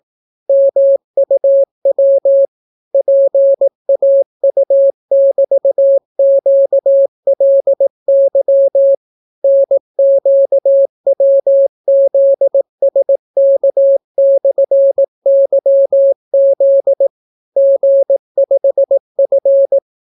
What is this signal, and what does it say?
Morse code, heard as a keyed tone, about 18 wpm, 550 Hz. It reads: MUW PAU<BT>QLY NQWZSK/YZ G5F